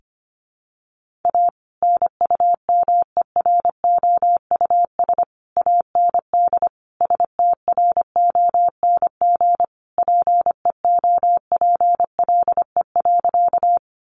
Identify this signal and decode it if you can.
Morse code, heard as a keyed tone, about 25 wpm, 700 Hz. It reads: A NUMEROUS AND STRONG PEOPLE.
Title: Sentences from Top 1400 Words 25wpm